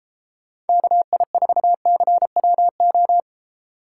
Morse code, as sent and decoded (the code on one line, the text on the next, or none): -.- .. ....- -.-. .-- ---
KI4CWO